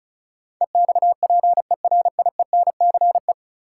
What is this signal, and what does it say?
Morse code, heard as a keyed tone, about 36 wpm, 700 Hz. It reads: EXPERIENCE
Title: Sentences from Top 600 Words - New Word Drill - 35wpm